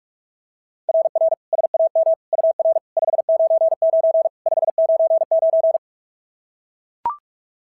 Morse code, heard as a keyed tone, about 41 wpm, 690 Hz. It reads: RR SAM UR 599 599
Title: SOTA QSO Practice 45wpm